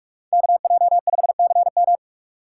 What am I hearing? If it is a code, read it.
Morse code, heard as a keyed tone, about 44 wpm, 700 Hz. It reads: KJ5CM